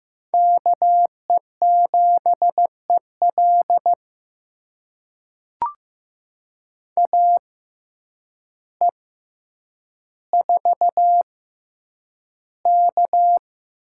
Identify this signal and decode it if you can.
Morse code, heard as a keyed tone, about 15 wpm, spaced as for 8 wpm, 700 Hz. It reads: KE7EL AE4K
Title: USA Call Signs - 5 Characters - 6x Character Spacing - 15wpm